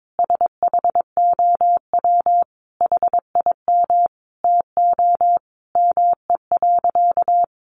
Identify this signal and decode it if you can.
Morse code, heard as a keyed tone, about 22 wpm, 700 Hz. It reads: SHOW HIM TO ME.
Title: Sentences from Top 200 Words 22wpm